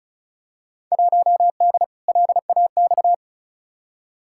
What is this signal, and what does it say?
Morse code, heard as a keyed tone, about 36 wpm, 700 Hz. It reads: 1D LAX
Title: Field Day Exchanges - 35wpm